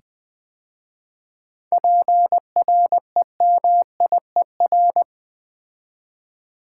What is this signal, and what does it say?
Morse code, heard as a keyed tone, about 20 wpm, 700 Hz. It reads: PREMIER
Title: 7 Letter Words - First 3 Letters Given - 20wpm